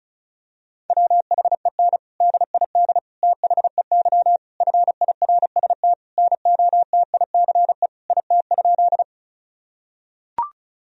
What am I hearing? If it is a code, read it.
Morse code, heard as a keyed tone, about 36 wpm, 700 Hz. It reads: WHEN DID THEY FIRST NOTICE IT?
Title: Sentences from Top 1000 Words - Review Part 2 - 35wpm